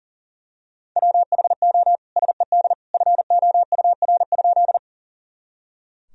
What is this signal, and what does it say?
Morse code, heard as a keyed tone, about 40 wpm, 700 Hz. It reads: WHO SED FOUR?
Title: Sentences from Top 300 Words and Abbreviations 40wpm